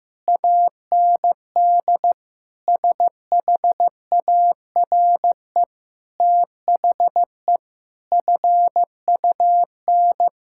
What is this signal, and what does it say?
Morse code, heard as a keyed tone, about 16 wpm, 700 Hz. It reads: AND SHARE THE FUN
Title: Sentences from Top 1000 Words - Review Part 6 - 15wpm